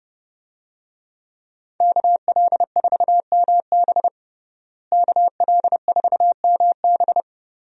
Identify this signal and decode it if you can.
Morse code, heard as a keyed tone, about 30 wpm, 700 Hz. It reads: KL4MB KL4MB
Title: USA Call Signs - 5 Characters - Repeated 2x - 30wpm